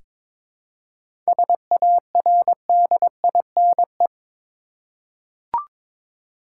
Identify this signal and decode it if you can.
Morse code, heard as a keyed tone, about 22 wpm, 700 Hz. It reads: SARDINE
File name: practice / 22wpm / Difficult Words with Hints - 22wpm